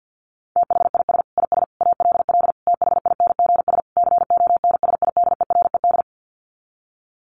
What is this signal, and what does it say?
Morse code, heard as a keyed tone, about 50 wpm, 700 Hz. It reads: THIS IS ALL THINGS CONSIDERED
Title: Sets of 5 Words 50wpm